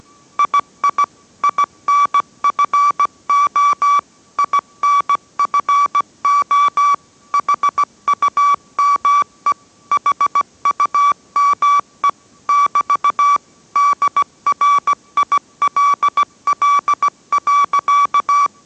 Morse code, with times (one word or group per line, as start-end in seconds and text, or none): 0.4-1.0 II
1.4-4.0 INFO
4.4-6.9 INFO
7.3-9.5 HUME
9.9-12.1 HUME
12.5-13.4 <BT>
13.8-18.5 DRILL.